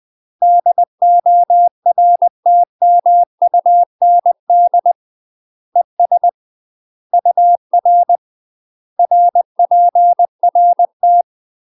Morse code, heard as a keyed tone, about 20 wpm, 700 Hz. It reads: DORTMUND ES UR RPRT
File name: UK MASTER QSOs TRAINING / 20WPM / UK MASTER QSOs 0028